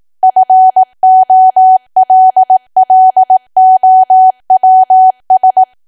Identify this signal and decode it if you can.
Morse code, heard as a keyed tone, about 18 wpm, 750 Hz. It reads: FOLLOWS
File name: cwa / W1AW-Level2 / 18WPM